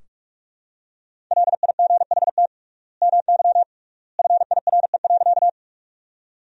Morse code, heard as a keyed tone, about 46 wpm, 700 Hz. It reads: LIGHT MY FIRE.
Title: 3x Word-Spacing - Sentences from Top 400 Words - 45wpm